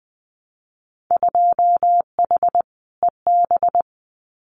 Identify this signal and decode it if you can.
Morse code, heard as a keyed tone, about 20 wpm, 700 Hz. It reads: 2H EB
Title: Winter Field Day Exchanges Encore 20wpm